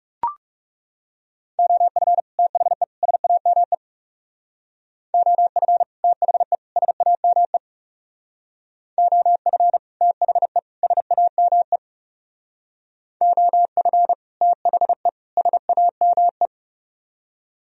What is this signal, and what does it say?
Morse code, heard as a keyed tone, about 36 wpm, 700 Hz. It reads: OF THE SAME OF THE SAME OF THE SAME OF THE SAME
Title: Speed-Racing - Sets of 3 Words 30wpm